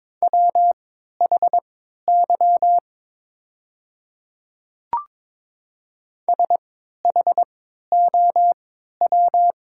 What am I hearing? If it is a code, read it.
Morse code, heard as a keyed tone, about 22 wpm, spaced as for 12 wpm, 700 Hz. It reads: WHY SHOW
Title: 3x Character Spacing - Top 200 Words - 22wpm